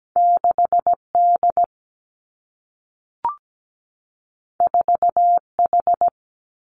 Morse code, heard as a keyed tone, about 17 wpm, 700 Hz. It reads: 6D 4H
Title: Two Challenging Letter-Number Combinations 17wpm